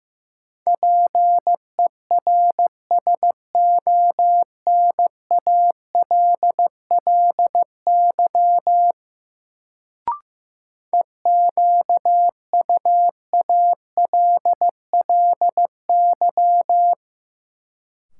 Morse code, heard as a keyed tone, about 15 wpm, 700 Hz. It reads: PERSONALLY EQUALLY
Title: Sentences from Top 1400 Words - New Word Drill - 15wpm